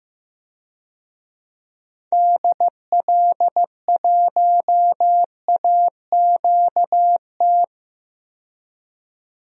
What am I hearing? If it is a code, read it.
Morse code, heard as a keyed tone, about 15 wpm, 700 Hz. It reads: DL1AQT
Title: DXCC-100 Call Signs - 6 Characters - 15wpm